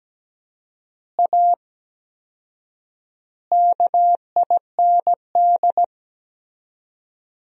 Morse code, heard as a keyed tone, about 17 wpm, 700 Hz. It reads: A KIND
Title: 4x Word-Spacing - Sentences from Top 500 Words - 17wpm